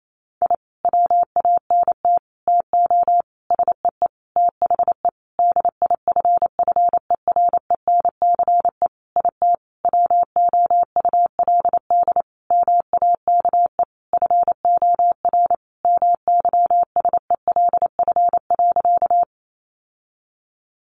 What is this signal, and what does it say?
Morse code, heard as a keyed tone, about 28 wpm, 700 Hz. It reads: I WANT TO SEE THE DIFFERENCE IT WOULD MAKE FOR MYSELF.
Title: Sentences from Top 1300 Words 28wpm